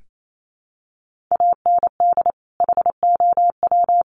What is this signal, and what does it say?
Morse code, heard as a keyed tone, about 28 wpm, 700 Hz. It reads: AND HOW
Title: Sentences from Top 1500 Words 28wpm